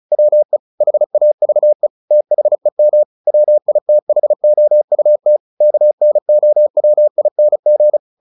Morse code, heard as a keyed tone, about 34 wpm, 600 Hz. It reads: WE HAVE THEM WITHOUT KNOWING
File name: the old man and the sea 2 35wpm